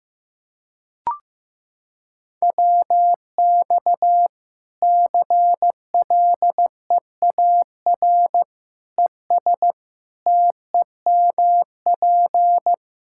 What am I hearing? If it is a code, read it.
Morse code, heard as a keyed tone, about 15 wpm, 700 Hz. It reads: WX CLEAR ES TEMP
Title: Practice QSO - Exchange 2 - Info - 15wpm